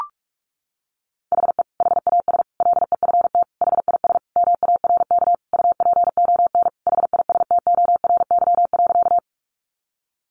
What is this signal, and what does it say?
Morse code, heard as a keyed tone, about 44 wpm, 700 Hz. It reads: HE HAS LEFT HIS MARK UPON HISTORY.
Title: Sentences from Top 1100 Words 45wpm